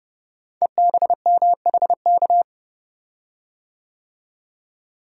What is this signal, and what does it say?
Morse code, heard as a keyed tone, about 28 wpm, 700 Hz. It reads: EBMHK
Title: Code Groups - 5 Letters - 30wpm